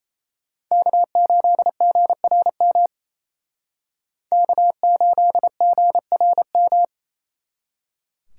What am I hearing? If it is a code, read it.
Morse code, heard as a keyed tone, about 34 wpm, 700 Hz. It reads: K8GRM K8GRM
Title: Speed Racing - USA 1x3 Call Signs 28wpm